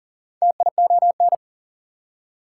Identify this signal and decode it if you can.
Morse code, heard as a keyed tone, about 40 wpm, 700 Hz. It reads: TION
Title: Prefixes and Suffixes 40wpm